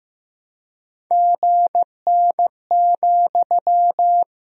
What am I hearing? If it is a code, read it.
Morse code, heard as a keyed tone, about 15 wpm, 700 Hz. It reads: GN,
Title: Sentences from Top 300 Words and Abbreviations 15wpm